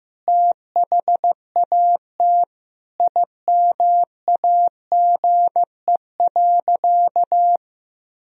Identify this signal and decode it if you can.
Morse code, heard as a keyed tone, about 15 wpm, 700 Hz. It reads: THAT IMAGE.
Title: Sentences from Top 1000 Words - Review Part 4 - 15wpm